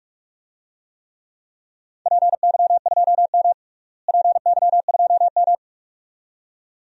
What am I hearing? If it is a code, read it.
Morse code, heard as a keyed tone, about 46 wpm, 700 Hz. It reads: PY2M PY2M
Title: DXCC-100 Call Signs - 4 Characters - Repeated 2x - 45wpm